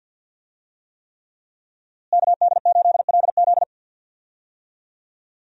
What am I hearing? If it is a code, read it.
Morse code, heard as a keyed tone, about 52 wpm, 700 Hz. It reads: KD8LZ
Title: USA Call Signs - 5 Characters - Repeated 3x - 50wpm